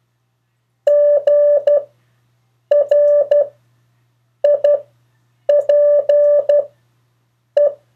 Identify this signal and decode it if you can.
Morse code, heard as a keyed tone, about 12 wpm, spaced as for 7 wpm, 580 Hz. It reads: GRIPE
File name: 2of4-words-1